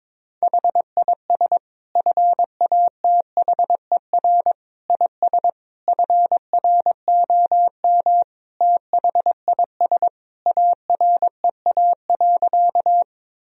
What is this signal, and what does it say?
Morse code, heard as a keyed tone, about 22 wpm, 700 Hz. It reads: HIS FATHER IS FROM THIS AREA.
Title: Sentences from Top 1000 Words - Review Part 2 - 22wpm